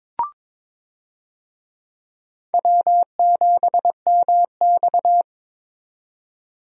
Morse code, heard as a keyed tone, about 22 wpm, 700 Hz. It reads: W7MX
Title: USA Call Signs - 4 Characters - 22wpm